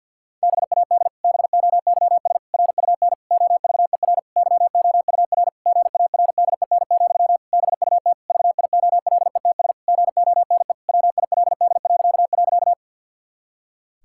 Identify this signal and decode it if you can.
Morse code, heard as a keyed tone, about 45 wpm, 700 Hz. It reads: BAD BOYS RUN OVER YOUR GARDEN, BUT VIOLETS GONE WILD..
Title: Pangram Challenge v2 - 50wpm